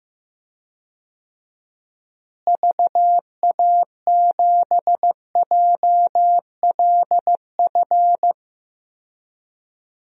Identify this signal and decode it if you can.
Morse code, heard as a keyed tone, about 15 wpm, 700 Hz. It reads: VA7JLF
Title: DXCC-100 Call Signs - 6 Characters - Repeated 2x - 15wpm